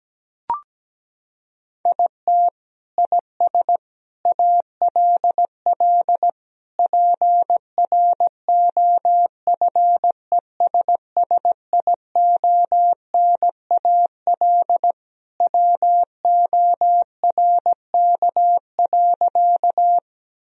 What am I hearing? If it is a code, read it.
Morse code, heard as a keyed tone, about 17 wpm, 700 Hz. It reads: IT IS ALL PROFESSIONAL WORK.
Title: Sentences from Top 1000 Words - Review Part 4 - 17wpm